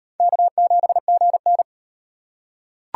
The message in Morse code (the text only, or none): K7GN